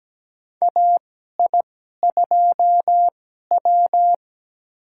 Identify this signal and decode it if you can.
Morse code, heard as a keyed tone, about 17 wpm, spaced as for 12 wpm, 700 Hz. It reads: AI2W
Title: USA Call Signs - 4 Characters - 2x Character Spacing - 17wpm